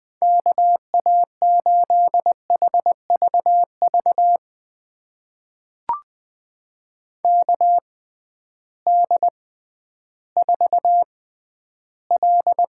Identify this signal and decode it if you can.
Morse code, heard as a keyed tone, about 21 wpm, spaced as for 12 wpm, 700 Hz. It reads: KA8HVV KD4L
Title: USA Call Signs - 6 Characters - 6x Character Spacing - 20wpm